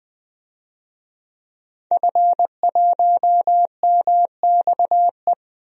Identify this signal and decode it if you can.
Morse code, heard as a keyed tone, about 20 wpm, 700 Hz. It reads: F1MXE
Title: DXCC-100 Call Signs - 5 Characters - Repeated 3x - 20wpm